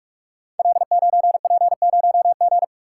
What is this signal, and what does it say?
Morse code, heard as a keyed tone, about 44 wpm, 700 Hz. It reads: L9P0G
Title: Code Groups - 5 Letter:Numbers - 45wpm